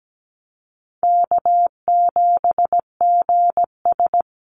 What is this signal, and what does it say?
Morse code, heard as a keyed tone, about 17 wpm, 700 Hz. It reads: K7GS
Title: Easy Call Signs Repeated 2x - 17wpm